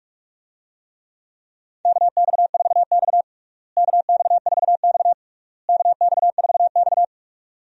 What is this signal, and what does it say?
Morse code, heard as a keyed tone, about 43 wpm, 700 Hz. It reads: KX4X KX4X KX4X
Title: USA Call Signs - 4 Characters - Repeated 3x - 45wpm